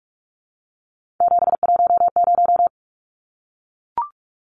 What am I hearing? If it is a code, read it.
Morse code, heard as a keyed tone, about 44 wpm, 700 Hz. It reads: 710
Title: Numbers - 3-Digit 45wpm